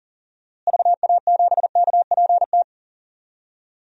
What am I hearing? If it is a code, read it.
Morse code, heard as a keyed tone, about 40 wpm, 700 Hz. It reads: VA7KPT
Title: DXCC-100 Call Signs - 6 Characters - Repeated 2x - 40wpm